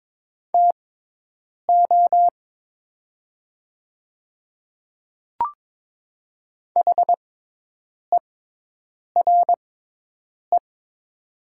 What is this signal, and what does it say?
Morse code, heard as a keyed tone, about 22 wpm, spaced as for 8 wpm, 700 Hz. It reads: TO HERE
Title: 6x Character Spacing - Top 200 Words - 22wpm